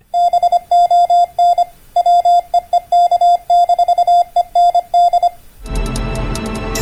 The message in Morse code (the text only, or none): BON WEEK-END